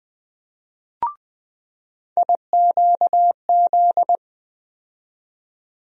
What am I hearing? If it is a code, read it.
Morse code, heard as a keyed tone, about 20 wpm, 700 Hz. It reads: IQZ
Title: USA Call Sign Postfixes - 3 Letters 20wpm